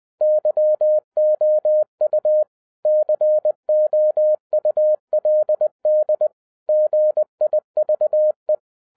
Morse code, heard as a keyed tone, about 20 wpm, 610 Hz. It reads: YOU COULD GIVE